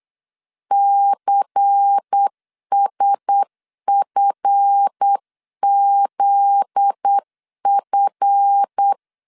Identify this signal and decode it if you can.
Morse code, heard as a keyed tone, about 8 wpm, 800 Hz. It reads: CSFZF
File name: CW / DET / 1 9 1500C S A 8wpm 5637